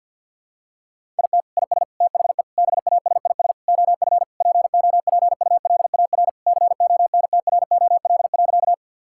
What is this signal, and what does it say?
Morse code, heard as a keyed tone, about 50 wpm, 700 Hz. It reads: IT IS THE BASIS OF POPULAR CONTROL.
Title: Sentences from Top 1000 Words - Review Part 5 - 50wpm